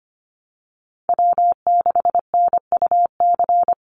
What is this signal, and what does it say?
Morse code, heard as a keyed tone, about 25 wpm, 700 Hz. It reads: W6NUC
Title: Call Signs Repeated 3x - 25wpm